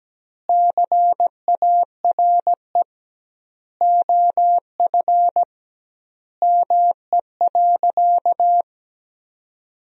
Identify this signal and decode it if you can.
Morse code, heard as a keyed tone, about 17 wpm, 700 Hz. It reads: CARE OF ME.
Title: 2x Word-Spacing - Sentences from Top 300 Words - 17wpm